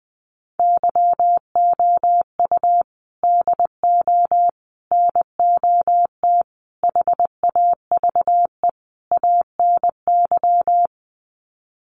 Morse code, heard as a keyed tone, about 20 wpm, 700 Hz. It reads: YOU DO NOT HAVE ANY
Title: Sets of 5 Words Encore 20wpm